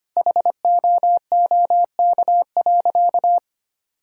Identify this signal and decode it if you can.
Morse code, heard as a keyed tone, about 25 wpm, 700 Hz. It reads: HOOK.
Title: Sentences from Top 1600 Words 25wpm